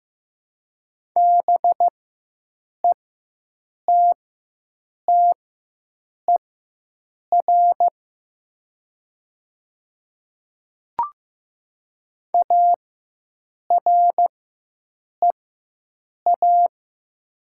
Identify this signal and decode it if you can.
Morse code, heard as a keyed tone, about 15 wpm, spaced as for 7 wpm, 700 Hz. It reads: BETTER AREA
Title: 4x Character Spacing - Top 500 Words - 15wpm